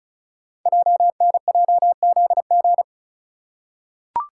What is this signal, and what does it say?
Morse code, heard as a keyed tone, about 36 wpm, 700 Hz. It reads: JNJZG
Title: Code Groups - 5 Letters - 35wpm